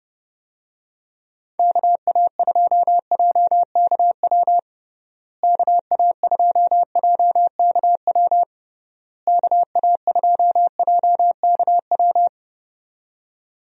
Morse code, heard as a keyed tone, about 32 wpm, 700 Hz. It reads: KA2JKW KA2JKW KA2JKW
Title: USA Call Signs - 6 Characters - Repeated 3x - 30wpm